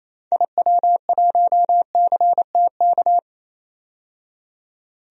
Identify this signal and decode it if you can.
Morse code, heard as a keyed tone, about 28 wpm, 700 Hz. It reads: IW1CTK
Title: DXCC-100 Call Signs - 6 Characters - 28wpm